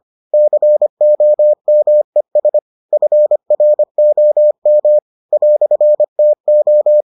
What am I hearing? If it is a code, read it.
Morse code, heard as a keyed tone, about 25 wpm, 600 Hz. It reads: COMES FROM "TO